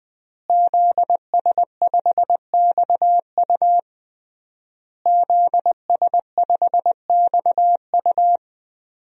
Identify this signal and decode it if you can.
Morse code, heard as a keyed tone, about 20 wpm, 700 Hz. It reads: ZS5XU ZS5XU
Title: DXCC-100 Call Signs - 5 Characters - Repeated 3x - 20wpm